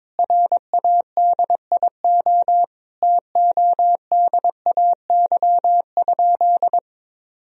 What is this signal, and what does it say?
Morse code, heard as a keyed tone, about 22 wpm, 700 Hz. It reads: RADIO TODAY?
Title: Sentences from Top 1000 Words - Review Part 3 - 22wpm